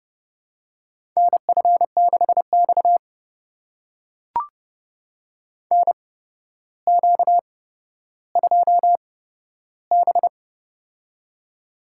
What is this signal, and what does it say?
Morse code, heard as a keyed tone, about 31 wpm, spaced as for 13 wpm, 700 Hz. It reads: NF6X NQ2B
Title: USA Call Signs - 4 Characters - 8x Character Spacing - 30wpm